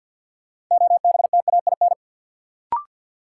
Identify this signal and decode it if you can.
Morse code, heard as a keyed tone, about 51 wpm, 700 Hz. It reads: OBTAIN
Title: Sentences from Top 1300 Words - New Word Drill - 50wpm